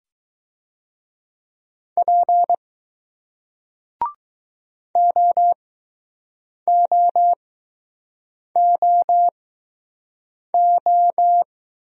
Morse code, heard as a keyed tone, about 20 wpm, spaced as for 6 wpm, 700 Hz. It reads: P OOOO